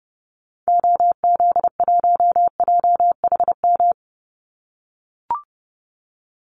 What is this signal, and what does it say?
Morse code, heard as a keyed tone, about 30 wpm, 700 Hz. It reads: OZ1JHM
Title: Call Signs - 30wpm